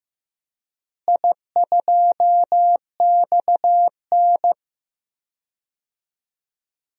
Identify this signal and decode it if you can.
Morse code, heard as a keyed tone, about 15 wpm, 700 Hz. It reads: I2XN
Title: DXCC-100 Call Signs - 4 Characters - Repeated 2x - 15wpm